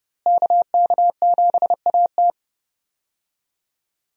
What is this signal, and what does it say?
Morse code, heard as a keyed tone, about 32 wpm, 700 Hz. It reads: KK7AT
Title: USA Call Signs - 5 Characters - Repeated 3x - 30wpm